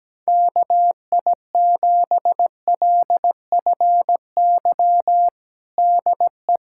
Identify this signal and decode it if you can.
Morse code, heard as a keyed tone, about 17 wpm, 700 Hz. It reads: KI7LFY DE
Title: USA Callsign DE Callsign 17wpm